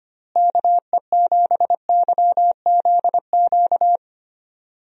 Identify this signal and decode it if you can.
Morse code, heard as a keyed tone, about 25 wpm, 700 Hz. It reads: KE7YZQ
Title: USA Call Signs - 6 Characters - Repeated 2x - 25wpm